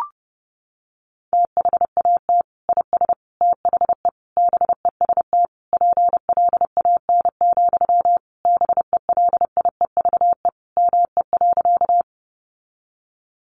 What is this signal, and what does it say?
Morse code, heard as a keyed tone, about 30 wpm, 700 Hz. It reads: THAT IS THE BEST PLAN, BELIEVE ME.
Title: Sentences from Top 1000 Words 30wpm